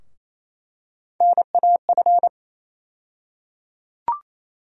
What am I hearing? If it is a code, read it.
Morse code, heard as a keyed tone, about 28 wpm, 700 Hz. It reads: NAF
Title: USA Call Sign Postfixes - 3 Letters 28wpm